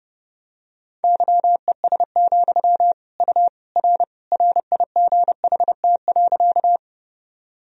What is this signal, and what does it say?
Morse code, heard as a keyed tone, about 30 wpm, 700 Hz. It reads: YES, U R RIGHT.
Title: Sentences from Top 500 Words and Abbreviations 30wpm